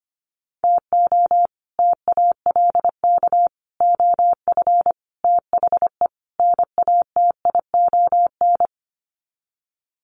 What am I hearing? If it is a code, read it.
Morse code, heard as a keyed tone, about 25 wpm, 700 Hz. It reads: TO TALK OF THE NATION
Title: Sets of 5 Words 25wpm